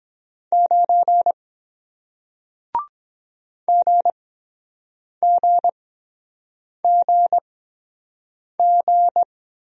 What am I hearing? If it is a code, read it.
Morse code, heard as a keyed tone, about 23 wpm, spaced as for 7 wpm, 700 Hz. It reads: 9 GGGG